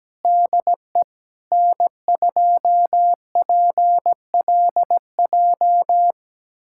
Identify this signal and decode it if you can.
Morse code, heard as a keyed tone, about 17 wpm, 700 Hz. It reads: DE N2PLJ
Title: USA Callsign DE Callsign 17wpm